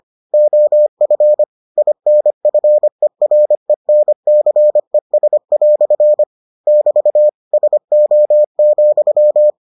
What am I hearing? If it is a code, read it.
Morse code, heard as a keyed tone, about 25 wpm, 600 Hz. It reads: OF INFERENCES" <BT> SO,